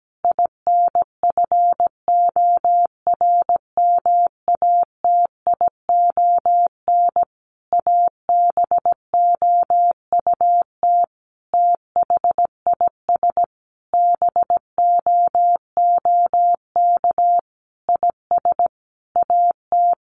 Morse code, heard as a keyed tone, about 17 wpm, 700 Hz. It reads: INFORMATION ABOUT THIS BOOK IS AT
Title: Sentences from Top 1200 Words 17wpm